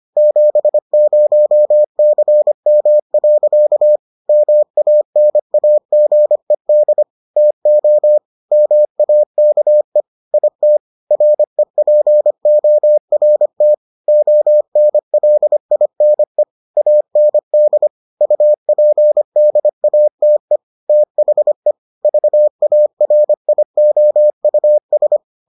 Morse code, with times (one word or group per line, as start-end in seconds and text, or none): 0.2-4.0 70CM.
4.3-7.0 MANAGED
7.4-8.2 TO
8.5-10.0 MAKE
10.3-10.8 IT
11.1-13.7 REPORT
14.1-16.4 ONLINE
16.8-17.9 AND
18.2-20.6 UPDATE
20.9-21.7 THE
22.0-25.2 VARIOUS